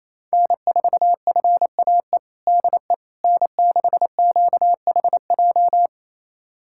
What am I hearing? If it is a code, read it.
Morse code, heard as a keyed tone, about 27 wpm, 700 Hz. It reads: N4FAE DE N6QHJ
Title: USA Callsign DE Callsign - 5 Characters - 28wpm